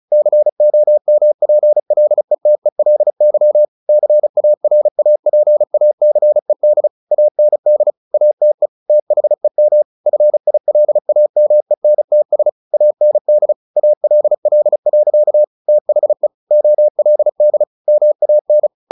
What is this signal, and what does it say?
Morse code, heard as a keyed tone, about 34 wpm, 600 Hz. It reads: COMPLETELY CARAPACED AND ATE THEM FILAMENTS AND ALL. THE OLD MAN
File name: the old man and the sea / the old man and the sea 1 35wpm